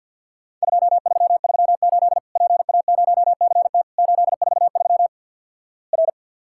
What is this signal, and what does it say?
Morse code, heard as a keyed tone, about 50 wpm, 700 Hz. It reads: 2338 PA0CT 843 R
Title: Worked All Europe DX Contest - QTC Exchange with Repeats - 50wpm